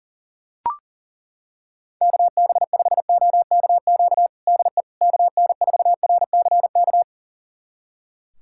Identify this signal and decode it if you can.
Morse code, heard as a keyed tone, about 40 wpm, 700 Hz. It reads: KB5OKQ DE KN4RCK
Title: USA Callsign DE Callsign 40wpm